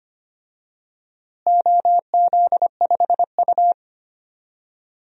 Morse code, heard as a keyed tone, about 25 wpm, 700 Hz. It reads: OZ5U
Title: DXCC-100 Call Signs - 4 Characters - Repeated 3x - 25wpm